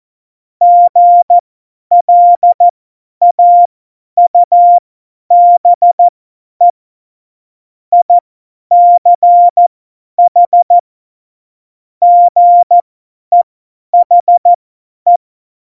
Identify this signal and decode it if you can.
Morse code, heard as a keyed tone, about 14 wpm, spaced as for 10 wpm, 700 Hz. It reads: GLAUBE ICH GEHE
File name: Morsekurs / Witze / Witz 31.Juli 2018, 50 F70